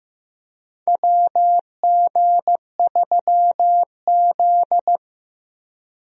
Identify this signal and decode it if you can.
Morse code, heard as a keyed tone, about 15 wpm, 700 Hz. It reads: WG3Z
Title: USA Call Signs - 4 Characters - Repeated 2x - 15wpm